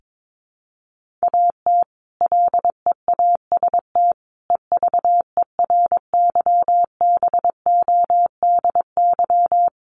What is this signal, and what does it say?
Morse code, heard as a keyed tone, about 22 wpm, 700 Hz. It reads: AT LEAST EVERYBODY
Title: Sentences from Top 1400 Words 22wpm